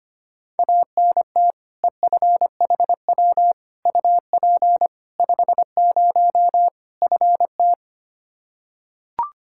Morse code, heard as a keyed tone, about 25 wpm, 700 Hz. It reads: ANT EFHW UP 50 FT
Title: Practice QSO - Exchange 2 - Info - 25wpm